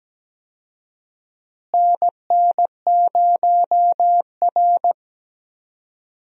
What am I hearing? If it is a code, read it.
Morse code, heard as a keyed tone, about 17 wpm, 700 Hz. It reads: NN0R
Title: USA Call Signs - 4 Characters - Repeated 2x - 17wpm